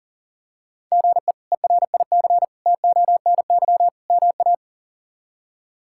Morse code, heard as a keyed tone, about 40 wpm, 700 Hz. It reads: GE ERIC TONY MA
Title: K1USN SST QSO Practice - 40wpm